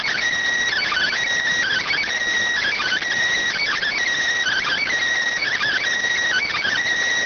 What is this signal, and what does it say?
Emergency Alert System (SAME) tones (header bursts), heard as harsh buzzing bursts. It is heard from 0.8 to 6.8 s.